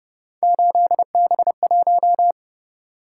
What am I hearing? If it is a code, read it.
Morse code, heard as a keyed tone, about 29 wpm, 700 Hz. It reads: 8B1
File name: Speed-Racing - Three Letter-Number Combinations - 25wpm